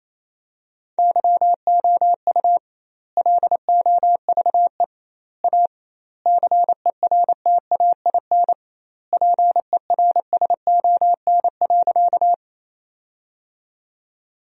Morse code, heard as a keyed tone, about 28 wpm, 700 Hz. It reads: YOU LOVE A CERTAIN PERSON.
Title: 2x Word-Spacing - Sentences from Top 500 Words - 28wpm